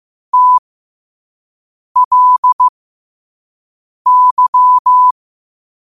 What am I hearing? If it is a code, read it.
Morse code, heard as a keyed tone, about 15 wpm, spaced as for 5 wpm, 1000 Hz. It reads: TLY
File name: qotdfiles / 5wpm